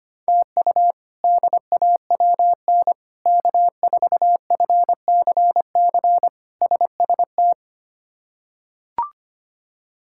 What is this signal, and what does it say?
Morse code, heard as a keyed tone, about 25 wpm, 700 Hz. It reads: TU DAWN K4FCC SST
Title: K1USN SST QSO Practice - 25wpm